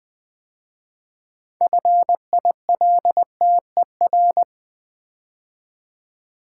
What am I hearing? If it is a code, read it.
Morse code, heard as a keyed tone, about 20 wpm, 700 Hz. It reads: FILTER
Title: 6 Letter Words - First 3 Letters Given - 20wpm